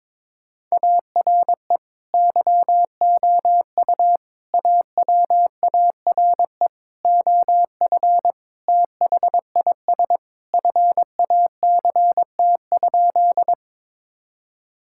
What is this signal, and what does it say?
Morse code, heard as a keyed tone, about 22 wpm, 700 Hz. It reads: ARE YOU AWARE OF THIS FACT?
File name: practice / 22wpm / Sentences with 2-Word Phrases - 22wpm